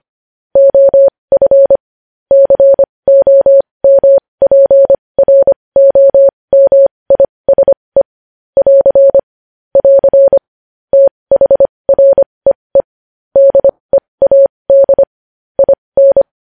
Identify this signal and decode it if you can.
Morse code, heard as a keyed tone, about 25 wpm, spaced as for 20 wpm, 550 Hz. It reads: OF COMPROMISE <AR> <AR> THREE DEAD IN
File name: podcast / News headlines-1x-25x20